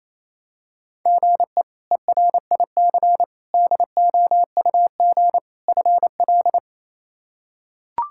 Morse code, heard as a keyed tone, about 28 wpm, 700 Hz. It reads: GE ERIC DOUG FL
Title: K1USN SST QSO Practice - 28wpm